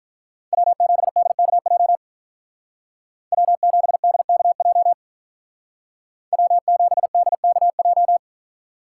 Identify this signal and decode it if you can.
Morse code, heard as a keyed tone, about 43 wpm, 700 Hz. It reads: W7DKJ W7DKJ W7DKJ
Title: Speed Racing - USA 1x3 Call Signs 35wpm